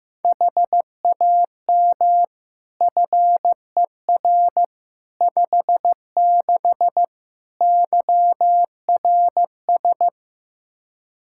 Morse code, heard as a keyed tone, about 16 wpm, 700 Hz. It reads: HAM FER 56 YRS
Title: Practice QSO - Exchange 3 - More Info - 15wpm